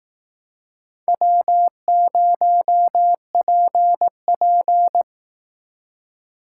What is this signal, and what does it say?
Morse code, heard as a keyed tone, about 18 wpm, 700 Hz. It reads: W0PP
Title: Speed Racing - USA 1x2 Call Signs 15wpm